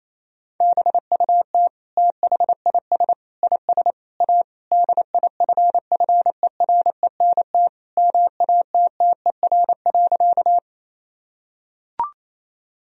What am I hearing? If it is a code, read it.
Morse code, heard as a keyed tone, about 28 wpm, 700 Hz. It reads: BUT THIS IS A DIFFERENT MATTER.
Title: Sentences from Top 1000 Words - Review Part 3 - 28wpm